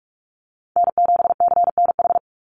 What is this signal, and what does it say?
Morse code, heard as a keyed tone, about 44 wpm, 700 Hz. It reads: N7CNH
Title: NA SOTA Chaser Callsigns 45wpm